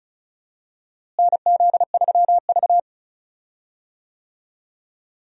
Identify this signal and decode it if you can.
Morse code, heard as a keyed tone, about 34 wpm, 700 Hz. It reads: NZ3V
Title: USA Call Signs - 4 Characters - Repeated 3x - 35wpm